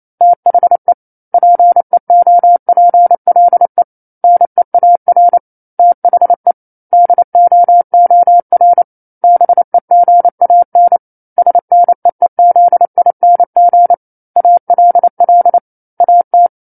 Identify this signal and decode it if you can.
Morse code, heard as a keyed tone, about 28 wpm, 700 Hz. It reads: THE PEOPLE NEAR THE DOOR BEGAN SNEEZING ALL AT
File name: 28 WPM